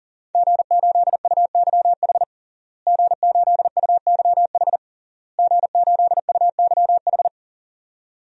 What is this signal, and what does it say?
Morse code, heard as a keyed tone, about 40 wpm, 700 Hz. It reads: G8UYH G8UYH G8UYH
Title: DXCC-100 Call Signs - 5 Characters - Repeated 3x - 40wpm